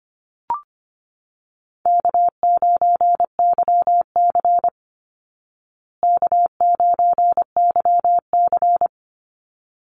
Morse code, heard as a keyed tone, about 24 wpm, 700 Hz. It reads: K9YC K9YC
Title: Easy Call Signs Repeated 2x - 25wpm